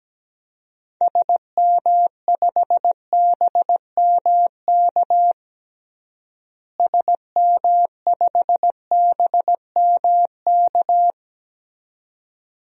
Morse code, heard as a keyed tone, about 17 wpm, 700 Hz. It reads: SM5BMK SM5BMK
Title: DXCC-100 Call Signs - 6 Characters - Repeated 2x - 17wpm